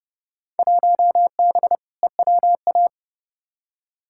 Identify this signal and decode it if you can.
Morse code, heard as a keyed tone, about 30 wpm, 700 Hz. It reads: 1B EWA
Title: Field Day Exchanges Encore - 30wpm